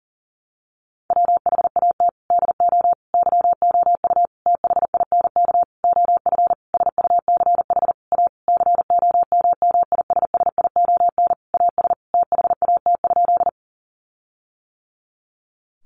Morse code, heard as a keyed tone, about 40 wpm, 700 Hz. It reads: WHAT DO YOU THINK OF SUCH A COMMISSION AS THAT?
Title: Sentences from Top 1400 Words 40wpm